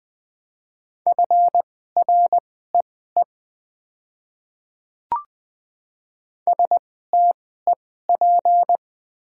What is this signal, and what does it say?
Morse code, heard as a keyed tone, about 20 wpm, spaced as for 14 wpm, 690 Hz. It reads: FREE STEP
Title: 2x Character Spacing - Top 500 Words - 20wpm